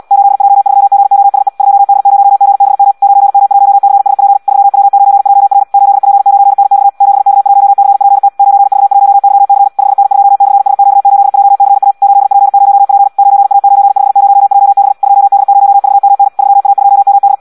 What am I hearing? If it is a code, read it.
Morse code, heard as a keyed tone, about 74 wpm, 800 Hz. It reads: 8J3GOSE 2D0GBM 9M1CSP 3W9JR 8P9NX 7K1CPT 9V1YC 5N2/SQ8JCA 8P0P 9A950JB 2D0VMN 3A2MD
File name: Audio practica CW / qrq-tx-74 2